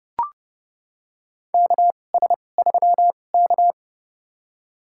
Morse code, heard as a keyed tone, about 30 wpm, spaced as for 22 wpm, 700 Hz. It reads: KS3K
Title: USA Call Signs - 2x Character Spacing - 30wpm